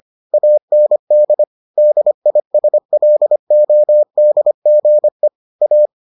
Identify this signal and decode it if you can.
Morse code, heard as a keyed tone, about 25 wpm, 600 Hz. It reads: AND DISLODGE A